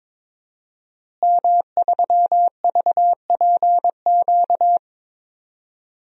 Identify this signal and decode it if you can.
Morse code, heard as a keyed tone, about 22 wpm, 700 Hz. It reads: M3VPQ
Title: DXCC-100 Call Signs - 5 Characters - Repeated 3x - 22wpm